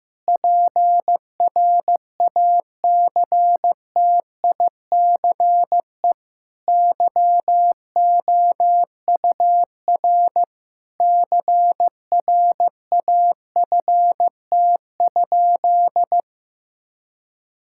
Morse code, heard as a keyed tone, about 15 wpm, 700 Hz. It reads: PRACTICE YOUR CRAFT?
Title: Sentences from Top 1000 Words - Review Part 6 - 15wpm